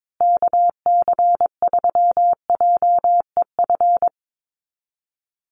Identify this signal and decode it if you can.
Morse code, heard as a keyed tone, about 22 wpm, 700 Hz. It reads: KC3JEF
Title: USA Call Signs Repeated 3x 22wpm